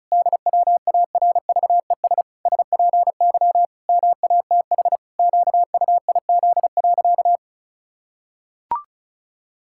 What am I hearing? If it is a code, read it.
Morse code, heard as a keyed tone, about 34 wpm, 700 Hz. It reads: DWARVES SPY MATH QUIZ.
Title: Pangram Challenge v2 - 35wpm